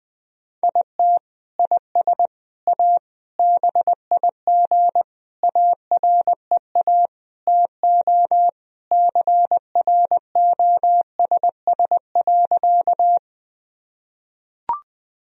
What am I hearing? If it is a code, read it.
Morse code, heard as a keyed tone, about 21 wpm, 700 Hz. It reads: IT IS A BIG AREA TO CROSS.
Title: Sentences from Top 1000 Words - Review Part 2 - 20wpm